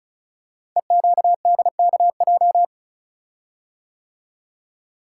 Morse code, heard as a keyed tone, about 35 wpm, 690 Hz. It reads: EQDKJ
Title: Code Groups - 5 Letters - 35wpm